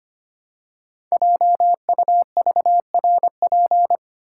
Morse code, heard as a keyed tone, about 25 wpm, 700 Hz. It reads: JUVRP
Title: Code Groups - 5 Letters - 25wpm